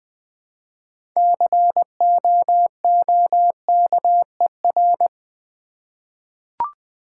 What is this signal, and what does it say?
Morse code, heard as a keyed tone, about 20 wpm, 700 Hz. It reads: COOKER
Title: 6 Letter Words - First 3 Letters Given - 20wpm